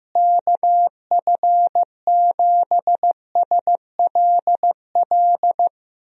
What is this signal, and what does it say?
Morse code, heard as a keyed tone, about 15 wpm, 700 Hz. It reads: KF7SLL
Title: USA Call Signs - 6 Characters - Repeated 2x - 15wpm